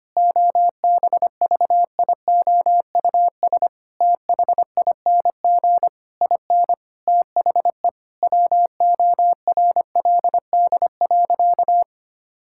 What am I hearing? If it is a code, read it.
Morse code, heard as a keyed tone, about 25 wpm, 700 Hz. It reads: OBVIOUS THING IN THE WORLD.
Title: Sentences from Top 1600 Words 25wpm